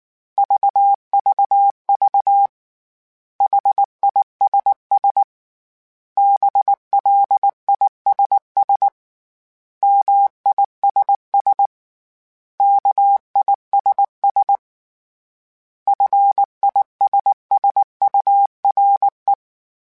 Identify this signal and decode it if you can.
Morse code, heard as a keyed tone, about 19 wpm, 800 Hz. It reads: VVV HISS BLISS MISS KISS FISSURE